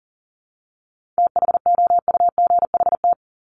Morse code, heard as a keyed tone, about 40 wpm, 700 Hz. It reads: THOUGHT
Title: Top 500 Words 40wpm